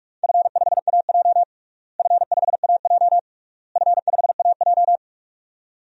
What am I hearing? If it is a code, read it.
Morse code, heard as a keyed tone, about 46 wpm, 700 Hz. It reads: F5AJ F5AJ F5AJ
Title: DXCC-100 Call Signs - 4 Characters - Repeated 3x - 45wpm